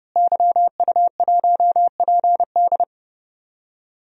Morse code, heard as a keyed tone, about 30 wpm, 700 Hz. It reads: YU1PD
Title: DXCC-100 Call Signs - 5 Characters - Repeated 2x - 30wpm